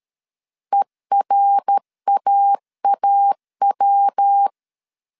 Morse code, heard as a keyed tone, about 13 wpm, 800 Hz. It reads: ERAAW
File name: CW / DET / 1 3 1500C S B 12wpm 2828